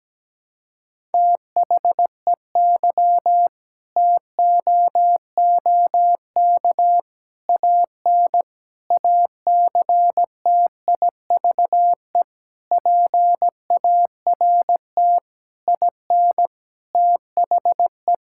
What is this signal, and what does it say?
Morse code, heard as a keyed tone, about 17 wpm, 700 Hz. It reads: THEY TOOK AN ACTIVE PART IN THE
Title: Sentences from Top 1000 Words - Review Part 6 - 17wpm